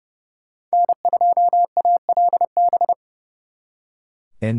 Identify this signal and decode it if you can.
Morse code, heard as a keyed tone, about 37 wpm, 700 Hz. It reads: N2ALB
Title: Speed Racing - USA 1x3 Call Signs 30wpm